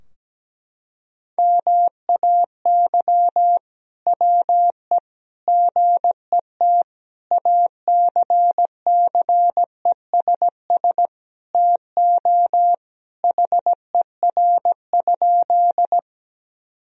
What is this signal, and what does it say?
Morse code, heard as a keyed tone, about 17 wpm, 700 Hz. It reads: MAY WE GET ACCESS TO HER?
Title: Sentences from Top 1000 Words - Review Part 4 - 17wpm